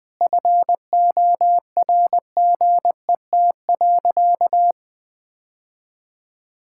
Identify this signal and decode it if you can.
Morse code, heard as a keyed tone, about 20 wpm, 700 Hz. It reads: FORGET.
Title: Sentences with 4-Word Phrases - 20wpm